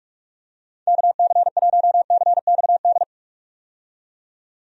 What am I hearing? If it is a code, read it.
Morse code, heard as a keyed tone, about 46 wpm, 700 Hz. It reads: KC1CXD
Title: USA Call Signs - 6 Characters - 45wpm